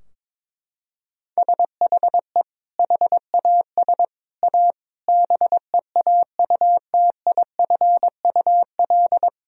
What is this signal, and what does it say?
Morse code, heard as a keyed tone, about 22 wpm, 700 Hz. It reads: SHE HAS A BEAUTIFUL
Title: Sentences from Top 1000 Words - Review Part 2 - 22wpm